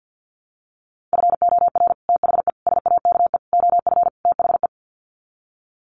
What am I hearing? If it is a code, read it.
Morse code, heard as a keyed tone, about 50 wpm, 700 Hz. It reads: FOR THE SAKE OF THE
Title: Sets of 5 Words Encore 50wpm